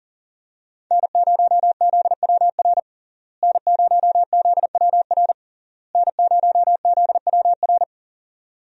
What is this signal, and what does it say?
Morse code, heard as a keyed tone, about 40 wpm, 700 Hz. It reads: N0ZWR N0ZWR N0ZWR
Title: USA Call Signs - 5 Characters - Repeated 3x - 40wpm